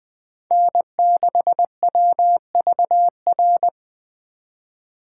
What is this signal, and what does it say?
Morse code, heard as a keyed tone, about 24 wpm, 700 Hz. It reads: N6WVR
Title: Speed Racing - USA Call Signs 20wpm